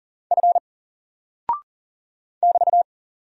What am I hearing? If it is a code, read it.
Morse code, heard as a keyed tone, about 40 wpm, 700 Hz. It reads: F <BT>
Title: Single Character (Letter:Number:Punctuation:Prosign) 40wpm